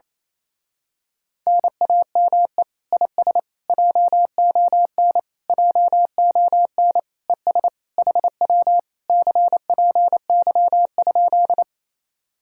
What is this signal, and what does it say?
Morse code, heard as a keyed tone, about 26 wpm, 700 Hz. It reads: NAME IS JON JON ES HW CPY?
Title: Speed-Racing Common QSO Phrases - 28wpm